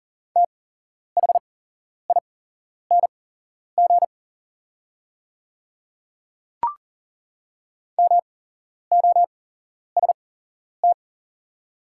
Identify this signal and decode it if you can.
Morse code, heard as a keyed tone, about 40 wpm, spaced as for 11 wpm, 700 Hz. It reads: THING MOST
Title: 8x Character Spacing - Top 100 Words - 40wpm